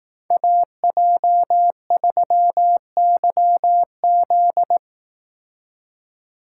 Morse code, heard as a keyed tone, about 18 wpm, 700 Hz. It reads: AJ3YZ
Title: Speed Racing - USA 2x2 Call Signs 15wpm